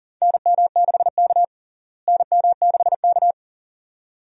Morse code, heard as a keyed tone, about 40 wpm, 700 Hz. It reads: NM6K NM6K
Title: USA Call Signs - 4 Characters - Repeated 2x - 40wpm